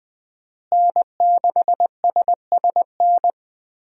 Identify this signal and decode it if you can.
Morse code, heard as a keyed tone, about 21 wpm, 700 Hz. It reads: N6SSN
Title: USA Call Signs - 5 Characters - Repeated 3x - 20wpm